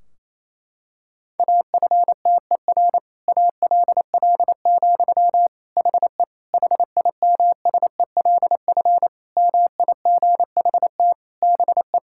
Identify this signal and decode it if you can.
Morse code, heard as a keyed tone, about 27 wpm, 700 Hz. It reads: AFTER ALL, HE HIMSELF MIGHT BE
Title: Sentences from Top 1600 Words 28wpm